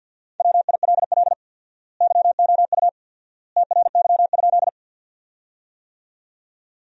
Morse code, heard as a keyed tone, about 50 wpm, 700 Hz. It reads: WILL YOU TRY?
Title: 4x Word-Spacing - Sentences from Top 200 Words - 50wpm